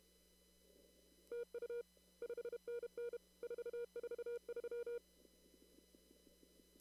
Morse code, heard as a keyed tone, about 32 wpm, 480 Hz. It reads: TU 5NN 443
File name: CW / kt5j wpx cw-op-k5pi